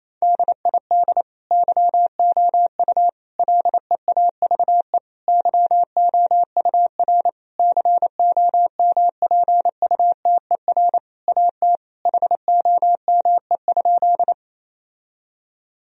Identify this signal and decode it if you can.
Morse code, heard as a keyed tone, about 28 wpm, 700 Hz. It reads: DID YOU LEAVE YOUR COMPUTER AT HOME?
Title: Sentences from Top 1000 Words - Review Part 3 - 28wpm